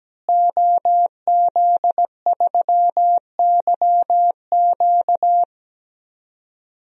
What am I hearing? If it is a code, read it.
Morse code, heard as a keyed tone, about 17 wpm, 700 Hz. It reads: OZ3YQ
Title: DXCC-100 Call Signs - 5 Characters - Repeated 2x - 17wpm